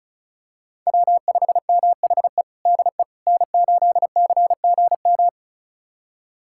Morse code, heard as a keyed tone, about 34 wpm, 700 Hz. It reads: W5MHE DE N8CGM
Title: USA Callsign DE Callsign - 5 Characters - 35wpm